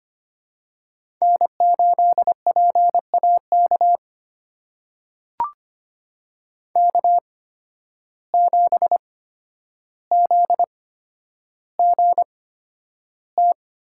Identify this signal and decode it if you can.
Morse code, heard as a keyed tone, about 26 wpm, spaced as for 11 wpm, 700 Hz. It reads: N8PAK K7ZGT
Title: USA Call Signs - 5 Characters - 8x Character Spacing - 25wpm